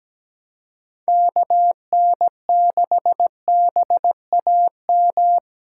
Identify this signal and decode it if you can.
Morse code, heard as a keyed tone, about 17 wpm, 700 Hz. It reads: KN6BAM
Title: USA Callsign DE Callsign - 6 Characters - 17wpm